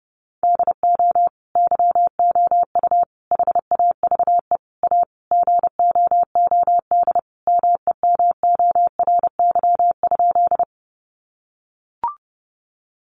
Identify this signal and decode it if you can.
Morse code, heard as a keyed tone, about 30 wpm, 700 Hz. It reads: DO YOU HAVE A GOOD MEMORY?
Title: Sentences from Top 1300 Words 30wpm